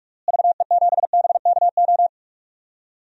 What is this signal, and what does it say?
Morse code, heard as a keyed tone, about 46 wpm, 700 Hz. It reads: VE7BKO
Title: DXCC-100 Call Signs - 6 Characters - Repeated 3x - 45wpm